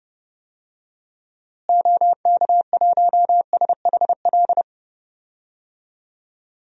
Morse code, heard as a keyed tone, about 30 wpm, 700 Hz. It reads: OK1SHL
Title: DXCC-100 Call Signs - 6 Characters - Repeated 3x - 30wpm